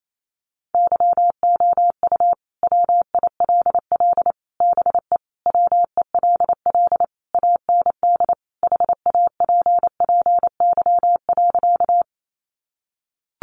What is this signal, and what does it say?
Morse code, heard as a keyed tone, about 28 wpm, 700 Hz. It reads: YOU WILL BE WELL AND HAPPY.
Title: Sentences from Top 1000 Words 28wpm